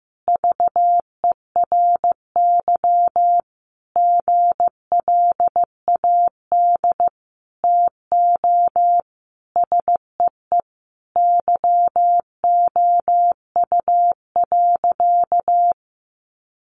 Morse code, heard as a keyed tone, about 15 wpm, 700 Hz. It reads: VERY GLAD TO SEE YOU.
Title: Sentences from Top 800 Words 15wpm